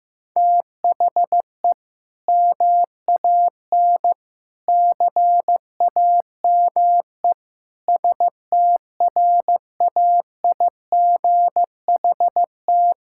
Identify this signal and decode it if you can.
Morse code, heard as a keyed tone, about 16 wpm, 700 Hz. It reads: THE MAN CAME STRAIGHT
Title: Sentences from Top 1000 Words - Review Part 6 - 15wpm